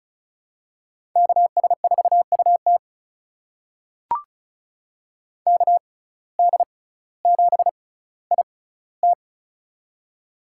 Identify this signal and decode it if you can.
Morse code, heard as a keyed tone, about 34 wpm, spaced as for 18 wpm, 700 Hz. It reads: KS4UT KD7IT
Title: USA Call Signs - 5 Characters - 6x Character Spacing - 35wpm